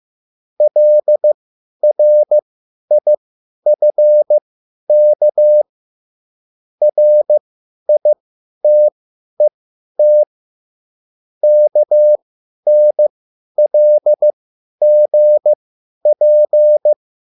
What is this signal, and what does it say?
Morse code, heard as a keyed tone, about 15 wpm, spaced as for 10 wpm, 600 Hz. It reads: LRIFK RITET KNLGP